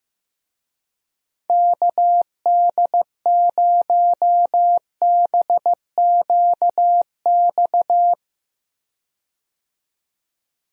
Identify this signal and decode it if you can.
Morse code, heard as a keyed tone, about 15 wpm, 700 Hz. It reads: KD0BQX
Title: USA Call Signs - 6 Characters - 15wpm